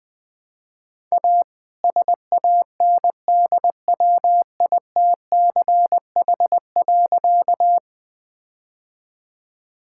Morse code, heard as a keyed tone, about 20 wpm, 700 Hz. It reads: A SANDWITCH.
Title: All-Age Appropriate Jokes - 20wpm